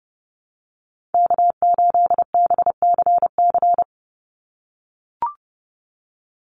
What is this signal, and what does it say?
Morse code, heard as a keyed tone, about 30 wpm, 700 Hz. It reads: K8BCC
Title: USA Call Signs 30wpm